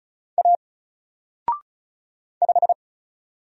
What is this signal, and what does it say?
Morse code, heard as a keyed tone, about 36 wpm, 700 Hz. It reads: A 5
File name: Single-Letter-Number-35wpm